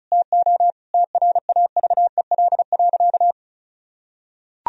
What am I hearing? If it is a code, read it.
Morse code, heard as a keyed tone, about 36 wpm, 700 Hz. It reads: TO TRAVEL.
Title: Sentences from Top 1000 Words - Review Part 2 - 35wpm